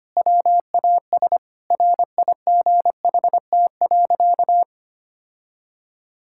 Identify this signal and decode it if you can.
Morse code, heard as a keyed tone, about 25 wpm, 700 Hz. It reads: WAS RIGHT.
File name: practice / 25wpm / Sentences with 4-Word Phrases - 25wpm